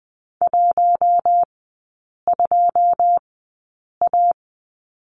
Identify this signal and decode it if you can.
Morse code, heard as a keyed tone, about 20 wpm, spaced as for 8 wpm, 690 Hz. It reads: 12A
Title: 145 Lesson 24e - Letter y - Three Characters 20wpm